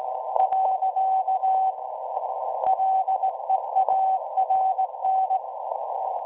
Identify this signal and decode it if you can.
Morse code, heard as a keyed tone, about 19 wpm, 760 Hz. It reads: . LEARN
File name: WebSDR records / SAQ at PI4THT part od 1005z 17-2kHz